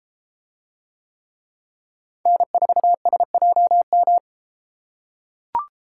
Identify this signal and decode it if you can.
Morse code, heard as a keyed tone, about 33 wpm, 700 Hz. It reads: N4SJM